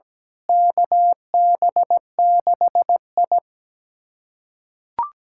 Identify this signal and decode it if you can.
Morse code, heard as a keyed tone, about 17 wpm, 700 Hz. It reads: KB6I
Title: USA Callsign DE Callsign - 4 Characters - 17wpm